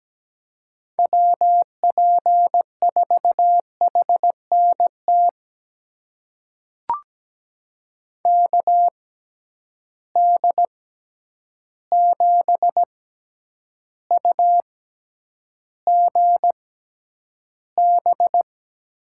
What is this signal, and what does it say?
Morse code, heard as a keyed tone, about 17 wpm, spaced as for 9 wpm, 700 Hz. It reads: WP4HNT KD7UGB
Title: USA Call Signs - 6 Characters - 6x Character Spacing - 17wpm